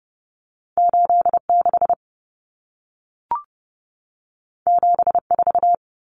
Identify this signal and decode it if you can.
Morse code, heard as a keyed tone, about 30 wpm, 700 Hz. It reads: 86 74
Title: Common Cut Numbers - 2-Digit 30wpm